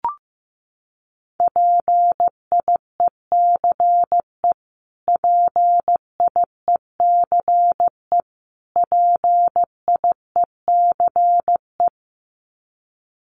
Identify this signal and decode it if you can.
Morse code, heard as a keyed tone, about 14 wpm, 700 Hz. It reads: PIECE PIECE PIECE
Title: Top 500 Words Repeated 15wpm